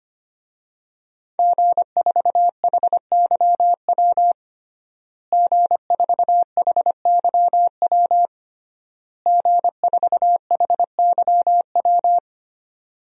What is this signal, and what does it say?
Morse code, heard as a keyed tone, about 25 wpm, 700 Hz. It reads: G4HYW G4HYW G4HYW
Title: DXCC-100 Call Signs - 5 Characters - Repeated 3x - 25wpm